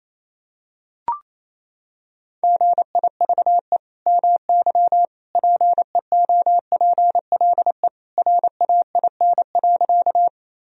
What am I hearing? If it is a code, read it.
Morse code, heard as a keyed tone, about 28 wpm, 700 Hz. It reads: GIVE MY PEOPLE RAIN.
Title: Sentences from Top 1000 Words - Review Part 1 - 28wpm